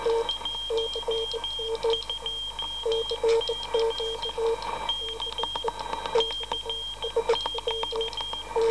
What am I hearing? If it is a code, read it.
Morse code, heard as a keyed tone, about 15 wpm, 3200 Hz, with noise. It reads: 12C K CQ